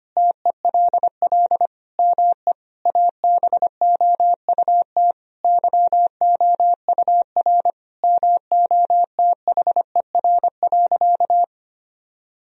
Morse code, heard as a keyed tone, about 25 wpm, 700 Hz. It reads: TELL ME ABOUT YOUR MOTHER.
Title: Sentences from Top 1000 Words - Review Part 3 - 25wpm